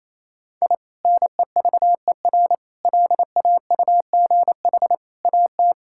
Morse code, heard as a keyed tone, about 28 wpm, 700 Hz. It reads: I NEVER LAUGH AT
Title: Sentences from Top 1000 Words - Review Part 6 - 28wpm